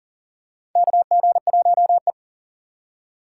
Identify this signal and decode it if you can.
Morse code, heard as a keyed tone, about 40 wpm, 700 Hz. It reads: KG1E
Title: USA Call Signs - 4 Characters - 3x Character Spacing - 40wpm